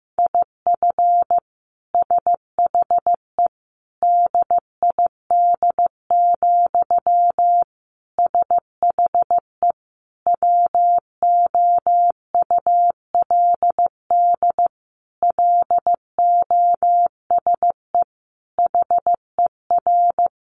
Morse code, heard as a keyed tone, about 15 wpm, 700 Hz. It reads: IF SHE DID, SHE WOULD LOSE HER
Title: Sentences from Top 1400 Words 15wpm